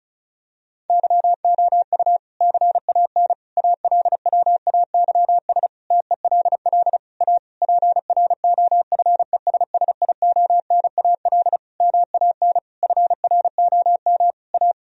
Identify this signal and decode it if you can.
Morse code, heard as a keyed tone, about 34 wpm, 700 Hz. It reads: YOU CAN ALWAYS TELL A PROFESSIONAL MAN FROM A
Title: Sentences from Top 1000 Words - Review Part 4 - 35wpm